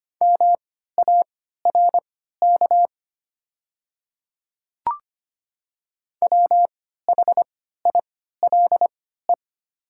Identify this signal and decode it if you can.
Morse code, heard as a keyed tone, about 25 wpm, spaced as for 14 wpm, 700 Hz. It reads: MARK WHILE
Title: 3x Character Spacing - Top 300 Words - 25wpm